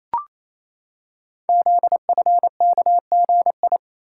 Morse code, heard as a keyed tone, about 27 wpm, 700 Hz. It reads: ZFKGI
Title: Code Groups - 5 Letters - 28wpm